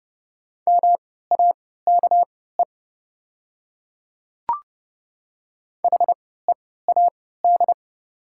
Morse code, heard as a keyed tone, about 30 wpm, spaced as for 17 wpm, 700 Hz. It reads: MAKE HEAD
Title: 3x Character Spacing - Top 500 Words - 30wpm